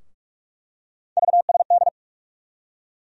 Morse code, heard as a keyed tone, about 46 wpm, 700 Hz. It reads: VSD
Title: USA Call Sign Postfixes - 3 Letters 45wpm